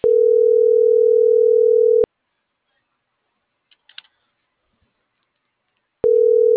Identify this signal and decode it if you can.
Ringback tone (North America), 440 + 480 Hz.